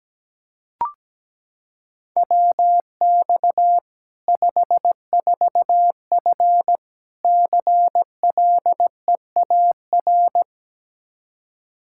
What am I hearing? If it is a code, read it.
Morse code, heard as a keyed tone, about 17 wpm, 700 Hz. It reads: WX 54F CLEAR
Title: K1USN SST QSO Practice 17wpm